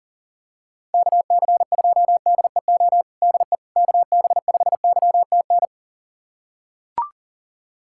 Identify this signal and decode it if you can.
Morse code, heard as a keyed tone, about 40 wpm, 700 Hz. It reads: KC2DEO DE KB5YTN
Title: USA Callsign DE Callsign 40wpm